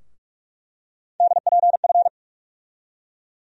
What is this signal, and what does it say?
Morse code, heard as a keyed tone, about 47 wpm, 700 Hz. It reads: DPF
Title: USA Call Sign Postfixes - 3 Letters 45wpm